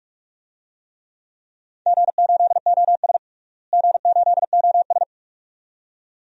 Morse code, heard as a keyed tone, about 42 wpm, 700 Hz. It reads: G8OS G8OS
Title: DXCC-100 Call Signs - 4 Characters - Repeated 2x - 45wpm